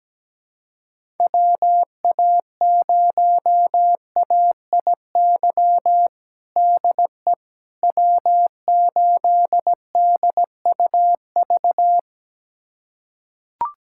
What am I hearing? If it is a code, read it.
Morse code, heard as a keyed tone, about 17 wpm, 700 Hz. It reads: WA0AIY DE W8DUV
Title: USA Callsign DE Callsign 17wpm